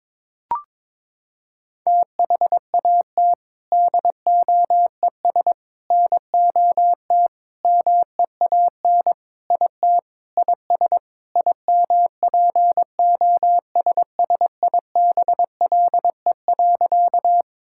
Morse code, heard as a keyed tone, about 22 wpm, 700 Hz. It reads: THAT DOES NOT MEAN IT IS IMPOSSIBLE.
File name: practice / 22wpm / Sentences with 4-Word Phrases - 22wpm